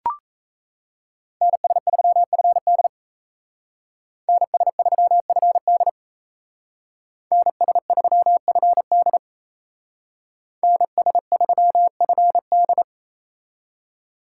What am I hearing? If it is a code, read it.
Morse code, heard as a keyed tone, about 34 wpm, 700 Hz. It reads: NS3FD NS3FD NS3FD NS3FD
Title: Speed Racing - USA 2x2 Call Signs 28wpm